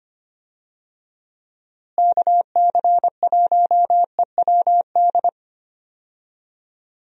Morse code, heard as a keyed tone, about 25 wpm, 700 Hz. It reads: KC1EWD
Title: USA Call Signs - 6 Characters - Repeated 2x - 25wpm